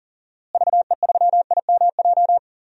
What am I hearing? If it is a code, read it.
Morse code, heard as a keyed tone, about 37 wpm, 700 Hz. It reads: VE3IMJ
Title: DXCC-100 Call Signs - 6 Characters - Repeated 2x - 40wpm